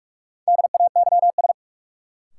Morse code, heard as a keyed tone, about 46 wpm, 700 Hz. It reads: DAYS
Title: Sets of 5 Words Encore - New Word Drill - 45wpm